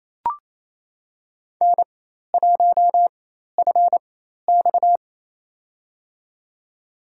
Morse code, heard as a keyed tone, about 27 wpm, spaced as for 13 wpm, 700 Hz. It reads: N1FX
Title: USA Call Signs - 4 Characters - 4x Character Spacing - 28wpm